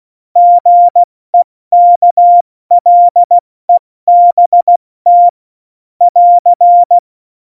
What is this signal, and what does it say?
Morse code, heard as a keyed tone, about 16 wpm, 700 Hz. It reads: GEKLEBT <AR>
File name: Morsekurs / Witze / Witz 10. September 2019, 70 F80